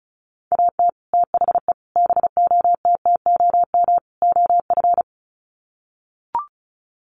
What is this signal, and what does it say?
Morse code, heard as a keyed tone, about 34 wpm, 700 Hz. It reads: AT THE BOTTOM OF
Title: Sets of 4 Words 35wpm